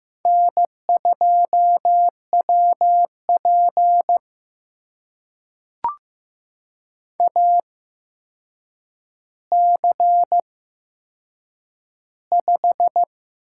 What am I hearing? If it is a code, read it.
Morse code, heard as a keyed tone, about 15 wpm, spaced as for 7 wpm, 700 Hz. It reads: N2WP AC5
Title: USA Call Signs - 4 Characters - 8x Character Spacing - 15wpm